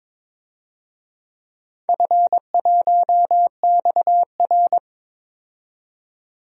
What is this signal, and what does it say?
Morse code, heard as a keyed tone, about 22 wpm, 700 Hz. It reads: F1XR
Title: DXCC-100 Call Signs - 4 Characters - 22wpm